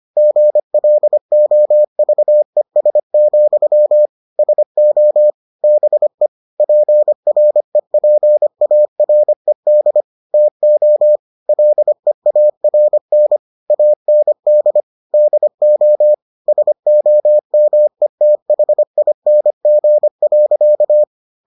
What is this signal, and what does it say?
Morse code, heard as a keyed tone, about 25 wpm, 600 Hz. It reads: GLOVES, SO BE PREPARED TO LEARN AND DO SOMETHING.